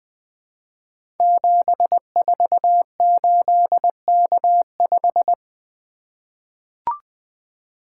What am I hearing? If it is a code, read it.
Morse code, heard as a keyed tone, about 20 wpm, 700 Hz. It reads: 748K5
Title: Code Groups - 5 Letter:Numbers - 20wpm